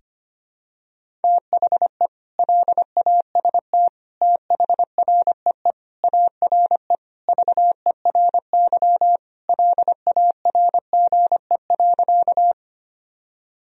Morse code, heard as a keyed tone, about 25 wpm, 700 Hz. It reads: THE LAST THREE ARE VERY LARGE.
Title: Sentences from Top 1000 Words - Review Part 1 - 25wpm